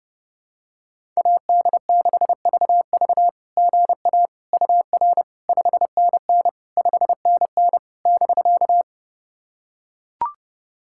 Morse code, heard as a keyed tone, about 32 wpm, 690 Hz. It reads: AD6VV GA UR 5NN 5NN <BK>
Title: SOTA QSO Practice 30wpm